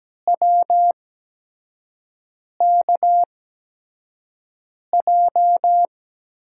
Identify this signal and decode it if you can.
Morse code, heard as a keyed tone, about 17 wpm, spaced as for 5 wpm, 700 Hz. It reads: WKJ
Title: USA Call Signs - 5 Characters - 8x Character Spacing - 17wpm